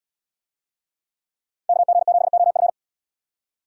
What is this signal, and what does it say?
Morse code, heard as a keyed tone, about 74 wpm, 700 Hz. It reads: KG7OF